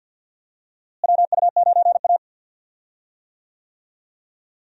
Morse code, heard as a keyed tone, about 52 wpm, 700 Hz. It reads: WU9A
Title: USA Call Signs - 4 Characters - Repeated 2x - 50wpm